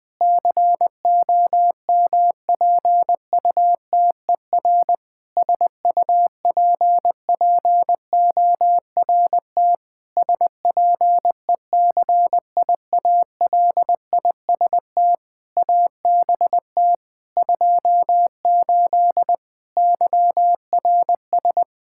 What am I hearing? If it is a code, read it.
Morse code, heard as a keyed tone, about 20 wpm, 700 Hz. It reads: COMPUTER SUPPORT SPECIALIST ABT 28 YRS
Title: Practice QSO - Exchange 3 - More Info - 20wpm